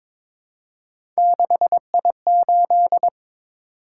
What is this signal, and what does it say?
Morse code, heard as a keyed tone, about 22 wpm, 700 Hz. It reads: 6I8
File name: practice / 22wpm / Three Letter-Number Combinations - 22wpm